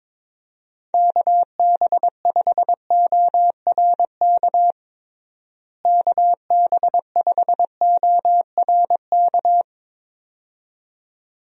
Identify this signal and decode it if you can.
Morse code, heard as a keyed tone, about 22 wpm, 700 Hz. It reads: KB5ORK KB5ORK
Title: USA Call Signs - 6 Characters - Repeated 2x - 22wpm